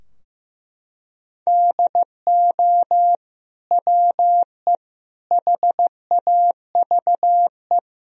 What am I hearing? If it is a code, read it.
Morse code, heard as a keyed tone, about 16 wpm, 700 Hz. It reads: DO WE HAVE
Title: Sentences from Top 1000 Words - Review Part 3 - 15wpm